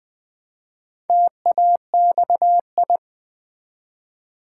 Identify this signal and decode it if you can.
Morse code, heard as a keyed tone, about 20 wpm, 700 Hz. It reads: TAXI
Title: Build A Word - 1 Letter At A Time - 4-Letter Words - 20wpm